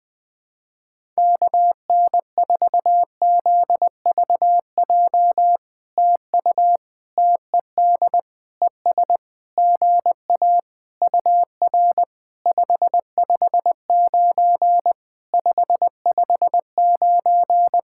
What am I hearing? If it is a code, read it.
Morse code, heard as a keyed tone, about 20 wpm, 700 Hz. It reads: KN4ZVJ TU TED ES GA UR 559 559
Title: POTA QSO Practice - 20wpm